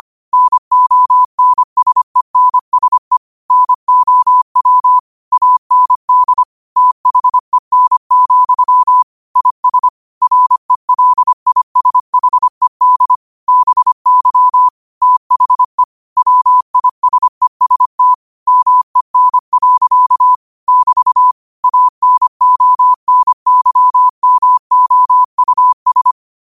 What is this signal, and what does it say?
Morse code, heard as a keyed tone, about 25 wpm, 1000 Hz. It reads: NONSENSE NOW AND THEN, IS RELISHED BY THE WISEST MEN. <BT> ANONYMOUS
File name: qotdfiles / 25wpm